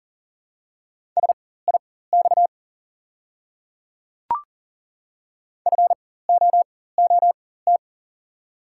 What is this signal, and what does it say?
Morse code, heard as a keyed tone, about 40 wpm, spaced as for 19 wpm, 700 Hz. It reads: SIX FOOT
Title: 4x Character Spacing - Top 500 Words - 40wpm